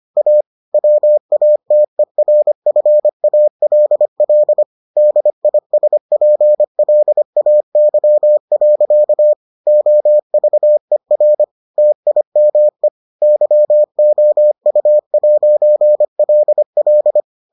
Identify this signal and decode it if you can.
Morse code, heard as a keyed tone, about 25 wpm, 600 Hz. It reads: A WATERFALL DISPLAY. OVER TIME YOU'LL